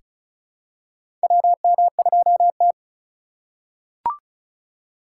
Morse code, heard as a keyed tone, about 36 wpm, 700 Hz. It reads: WM2T
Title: USA Call Signs - 4 Characters - 2x Character Spacing - 35wpm